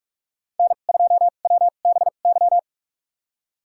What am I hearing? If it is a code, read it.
Morse code, heard as a keyed tone, about 47 wpm, spaced as for 33 wpm, 700 Hz. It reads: N2WBY
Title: USA Call Signs - 5 Characters - 2x Character Spacing - 45wpm